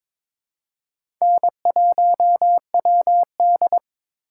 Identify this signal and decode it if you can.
Morse code, heard as a keyed tone, about 22 wpm, 700 Hz. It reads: N1WD
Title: USA Call Signs - 4 Characters - 2x Character Spacing - 22wpm